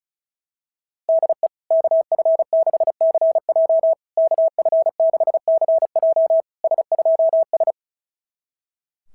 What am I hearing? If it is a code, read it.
Morse code, heard as a keyed tone, about 36 wpm, 650 Hz. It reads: DE KF6CJ KF6CJ S2S
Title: SOTA QSO Practice 35wpm